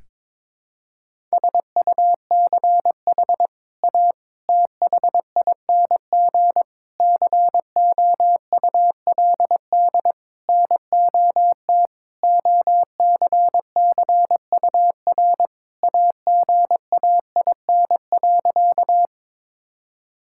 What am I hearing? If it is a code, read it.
Morse code, heard as a keyed tone, about 22 wpm, 700 Hz. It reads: SUCH A THING COULD NOT OCCUR AGAIN.
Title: Sentences from Top 1000 Words - Review Part 6 - 22wpm